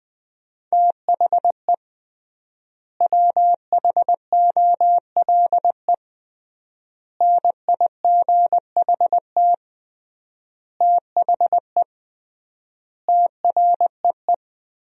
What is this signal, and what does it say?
Morse code, heard as a keyed tone, about 20 wpm, 700 Hz. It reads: THE WHOLE NIGHT THE TREE
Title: 3x Word-Spacing - Sentences from Top 500 Words - 20wpm